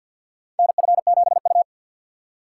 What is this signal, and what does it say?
Morse code, heard as a keyed tone, about 51 wpm, 700 Hz. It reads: NF7U
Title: USA Call Signs - 4 Characters - 6x Character Spacing - 50wpm